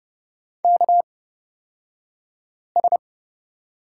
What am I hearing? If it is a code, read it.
Morse code, heard as a keyed tone, about 30 wpm, spaced as for 5 wpm, 700 Hz. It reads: KS
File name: Lesson 36d - BT - Two Characters 30wpm